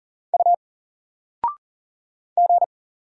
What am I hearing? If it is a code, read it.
Morse code, heard as a keyed tone, about 39 wpm, 700 Hz. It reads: U G
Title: Single Letter-Number Alt 40wpm